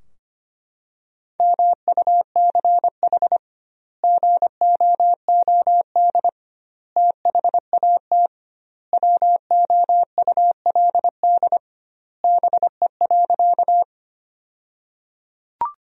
Morse code, heard as a keyed tone, about 25 wpm, 700 Hz. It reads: MUCH GOOD THAT WOULD BE.
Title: 2x Word-Spacing - Sentences from Top 200 Words - 25wpm